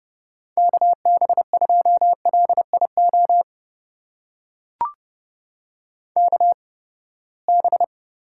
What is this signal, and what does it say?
Morse code, heard as a keyed tone, about 30 wpm, spaced as for 21 wpm, 700 Hz. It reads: KB2LIO KB
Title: USA Call Signs - 6 Characters - 8x Character Spacing - 30wpm